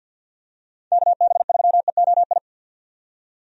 Morse code, heard as a keyed tone, about 48 wpm, 700 Hz. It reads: KB3EOI
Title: USA Call Signs - 6 Characters - 50wpm